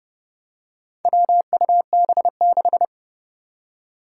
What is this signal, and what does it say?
Morse code, heard as a keyed tone, about 30 wpm, 700 Hz. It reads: WUB6
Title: Four Challenging Letter-Number Combinations 30wpm